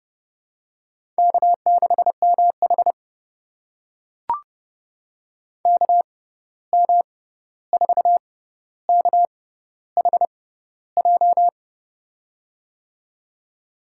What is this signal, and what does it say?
Morse code, heard as a keyed tone, about 30 wpm, spaced as for 14 wpm, 700 Hz. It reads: K6MH KM4KHJ
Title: USA Call Signs - 6x Character Spacing - 30wpm